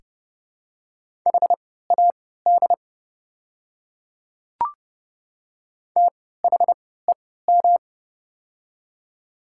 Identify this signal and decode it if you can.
Morse code, heard as a keyed tone, about 30 wpm, spaced as for 17 wpm, 700 Hz. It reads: HAD THEM
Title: 3x Character Spacing - Top 100 Words - 30wpm